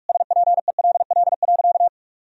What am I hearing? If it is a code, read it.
Morse code, heard as a keyed tone, about 46 wpm, 700 Hz. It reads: SPELL.
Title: Sentences from Top 1000 Words - Review Part 1 - 45wpm